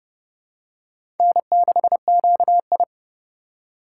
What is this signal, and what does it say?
Morse code, heard as a keyed tone, about 30 wpm, 700 Hz. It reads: N6QI
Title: USA Call Signs - 4 Characters - 4x Character Spacing - 30wpm